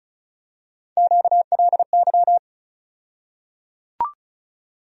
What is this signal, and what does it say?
Morse code, heard as a keyed tone, about 34 wpm, 700 Hz. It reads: QLY
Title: USA Call Sign Postfixes - 3 Letters 35wpm